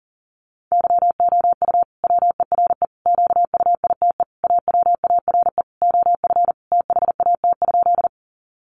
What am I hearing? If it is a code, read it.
Morse code, heard as a keyed tone, about 40 wpm, 700 Hz. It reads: YOU WERE QUITE AWARE OF THAT?
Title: Sentences from Top 1200 Words 40wpm